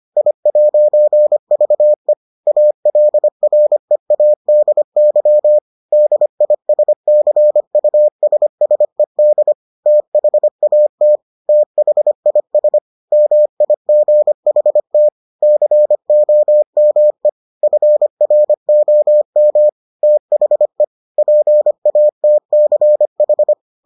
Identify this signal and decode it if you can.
Morse code, heard as a keyed tone, about 25 wpm, 600 Hz. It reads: I'VE ALREADY DISCUSSED THAT THIS MIGHT COME FROM THE PATCH